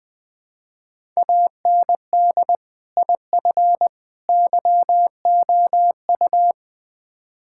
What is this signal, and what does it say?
Morse code, heard as a keyed tone, about 21 wpm, 700 Hz. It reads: AND IF YOU
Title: Build A Phrase - 3-Word Phrases - 20wpm